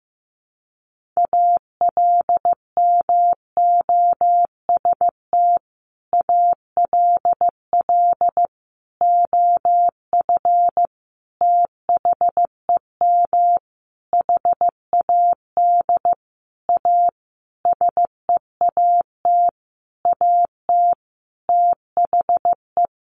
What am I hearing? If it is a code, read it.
Morse code, heard as a keyed tone, about 15 wpm, 700 Hz. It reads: ALMOST ALL OF THEM HAD A SEAT AT THE
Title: Sentences from Top 1500 Words 15wpm